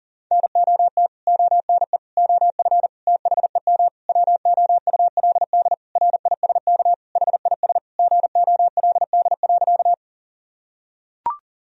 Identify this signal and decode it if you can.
Morse code, heard as a keyed tone, about 40 wpm, 700 Hz. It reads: NOT ONE OF THEM WOULD RISK HIS GOLD.
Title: Sentences from Top 1000 Words - Review Part 4 - 40wpm